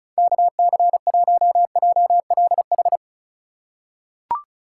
Morse code, heard as a keyed tone, about 36 wpm, 700 Hz. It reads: KC1JLH
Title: USA Call Signs - 6 Characters - 3x Character Spacing - 35wpm